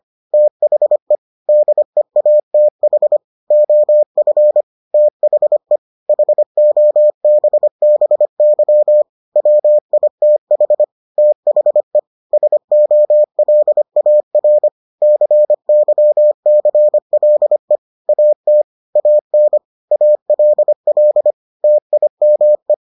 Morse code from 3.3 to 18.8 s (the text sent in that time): OF THE HOBBY WITH THE SOLAR CYCLE AT